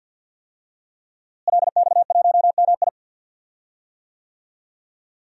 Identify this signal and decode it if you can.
Morse code, heard as a keyed tone, about 53 wpm, 700 Hz. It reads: LX1MI
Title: DXCC-100 Call Signs - 5 Characters - Repeated 2x - 50wpm